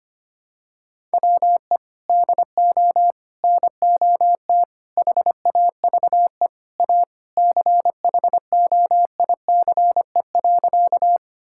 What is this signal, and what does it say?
Morse code, heard as a keyed tone, about 25 wpm, 700 Hz. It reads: WE DO NOT HAVE A CHOICE.
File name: practice / 25wpm / Sentences with 3-Word Phrases - 25wpm